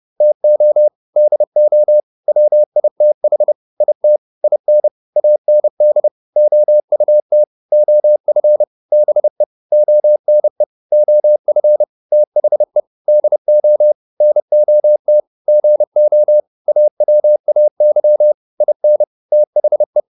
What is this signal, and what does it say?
Morse code, heard as a keyed tone, about 30 wpm, 600 Hz. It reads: TO DO WITH IT IN AND OUT OF BE ONE OF THE DO NOT GO AWAY IN THE